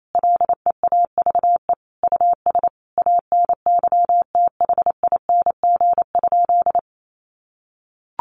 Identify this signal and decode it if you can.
Morse code, heard as a keyed tone, about 28 wpm, 700 Hz. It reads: LEAVE US ANYTHING?
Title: Sentences from Top 1100 Words 28wpm